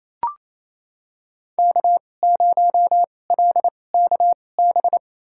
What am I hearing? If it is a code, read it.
Morse code, heard as a keyed tone, about 28 wpm, spaced as for 20 wpm, 700 Hz. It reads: K0LKB
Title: USA Call Signs - 2x Character Spacing - 28wpm